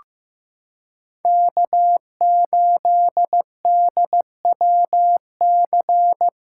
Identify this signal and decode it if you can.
Morse code, heard as a keyed tone, about 15 wpm, 700 Hz. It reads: K8DWC
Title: USA Callsign DE Callsign - 5 Characters - 15wpm